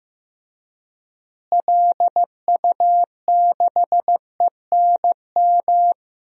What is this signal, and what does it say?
Morse code, heard as a keyed tone, about 15 wpm, 700 Hz. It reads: LU6ENM
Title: DXCC-100 Call Signs - 6 Characters - Repeated 3x - 15wpm